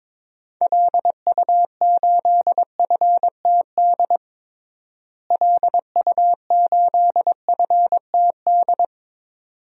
Morse code, heard as a keyed tone, about 22 wpm, 700 Hz. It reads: LU8FTD LU8FTD
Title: DXCC-100 Call Signs - 6 Characters - Repeated 2x - 22wpm